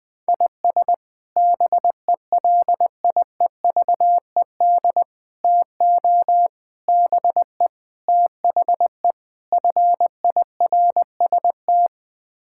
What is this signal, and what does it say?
Morse code, heard as a keyed tone, about 20 wpm, 700 Hz. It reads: IS BELIEVED TO BE THE FIRST
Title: Sets of 6 Words - 20wpm